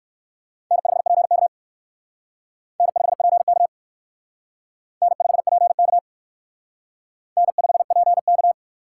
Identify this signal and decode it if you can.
Morse code, heard as a keyed tone, about 56 wpm, 700 Hz. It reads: N5PK N5PK N5PK N5PK
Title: Speed Racing - USA 1x2 Call Signs 45wpm